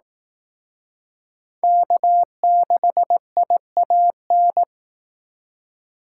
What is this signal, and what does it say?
Morse code, heard as a keyed tone, about 18 wpm, 700 Hz. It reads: K6IAN
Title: Speed Racing - USA 1x3 Call Signs 15wpm